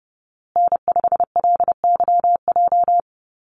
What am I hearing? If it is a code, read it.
Morse code, heard as a keyed tone, about 30 wpm, 700 Hz. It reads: N5LYJ
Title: Call Signs Repeated 2x - 30wpm